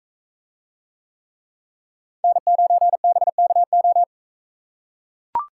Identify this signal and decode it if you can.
Morse code, heard as a keyed tone, about 42 wpm, 700 Hz. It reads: N9BKO